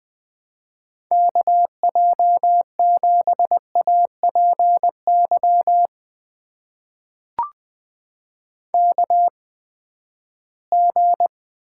Morse code, heard as a keyed tone, about 20 wpm, spaced as for 14 wpm, 700 Hz. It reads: KJ7APY KG
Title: USA Call Signs - 8x Character Spacing - 20wpm